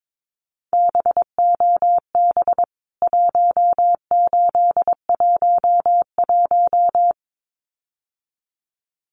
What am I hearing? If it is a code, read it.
Morse code, heard as a keyed tone, about 22 wpm, 700 Hz. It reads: BOB 1811
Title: CWT Exchange 22wpm